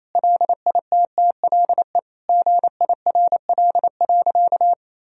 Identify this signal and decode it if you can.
Morse code, heard as a keyed tone, about 28 wpm, 700 Hz. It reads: LITTLE GIRL.
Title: Sentences from Top 1000 Words - Review Part 1 - 28wpm